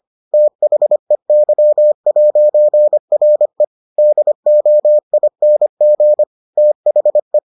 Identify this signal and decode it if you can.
Morse code, heard as a keyed tone, about 25 wpm, 600 Hz. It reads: THEY'RE DOING THE